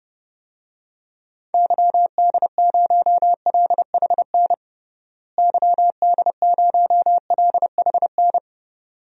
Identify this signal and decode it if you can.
Morse code, heard as a keyed tone, about 31 wpm, 700 Hz. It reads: YD0LHN YD0LHN
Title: DXCC-100 Call Signs - 6 Characters - Repeated 2x - 30wpm